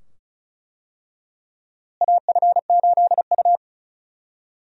Morse code, heard as a keyed tone, about 36 wpm, 700 Hz. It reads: AF8U
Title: USA Call Signs - 4 Characters - Repeated 2x - 35wpm